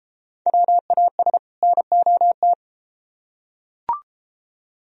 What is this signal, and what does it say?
Morse code, heard as a keyed tone, about 33 wpm, 700 Hz. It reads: WAS NOT